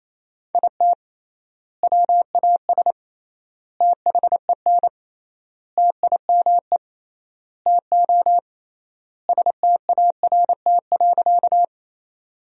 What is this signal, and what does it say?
Morse code, heard as a keyed tone, about 28 wpm, 700 Hz. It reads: IT WAS THEN TIME TO START.
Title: 3x Word-Spacing - Sentences from Top 300 Words - 28wpm